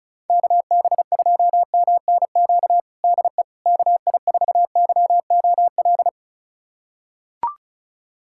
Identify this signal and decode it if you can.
Morse code, heard as a keyed tone, about 36 wpm, 700 Hz. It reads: KB2MNQ DE KI4YOL
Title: USA Callsign DE Callsign - 6 Characters - 35wpm